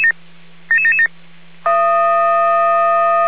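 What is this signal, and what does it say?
Five-tone selective call (ZVEI), heard as quick runs of five tones, tones of about 70 ms. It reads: 77777
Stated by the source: ZVEI five-tone call 99999 received by SDR.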